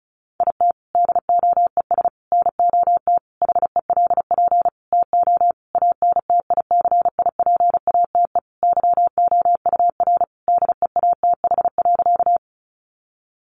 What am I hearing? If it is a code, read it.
Morse code, heard as a keyed tone, about 34 wpm, 700 Hz. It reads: IT DOES NOT HELP TO ANTICIPATE YOUR DEATH.
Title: Sentences from Top 1500 Words 35wpm